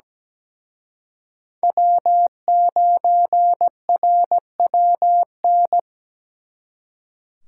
Morse code, heard as a keyed tone, about 20 wpm, 700 Hz. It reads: W9RWN
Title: Speed Racing - USA 1x3 Call Signs 17wpm